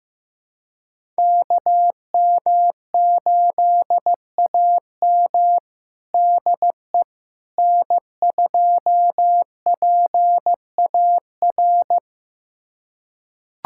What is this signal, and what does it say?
Morse code, heard as a keyed tone, about 15 wpm, 700 Hz. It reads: KM8AM DE N2PAR
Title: USA Callsign DE Callsign - 5 Characters - 15wpm